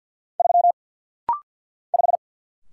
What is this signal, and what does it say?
Morse code, heard as a keyed tone, about 50 wpm, 700 Hz. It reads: 3 5
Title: Single Letter-Number Alt - Rapid-Fire 50wpm